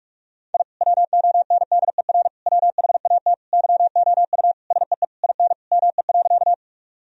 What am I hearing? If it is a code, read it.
Morse code, heard as a keyed tone, about 46 wpm, 700 Hz. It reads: I WONDER WHAT YOU SEE IN ME.
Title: Sentences from Top 1000 Words - Review Part 3 - 45wpm